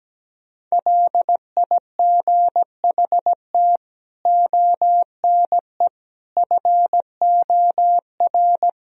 Morse code, heard as a keyed tone, about 17 wpm, 700 Hz. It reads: LIGHT ONE FOR
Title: Sentences from Top 1000 Words - Review Part 1 - 17wpm